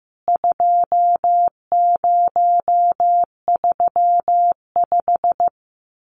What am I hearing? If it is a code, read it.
Morse code, heard as a keyed tone, about 15 wpm, 700 Hz. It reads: 2035
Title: CWT Exchange 15wpm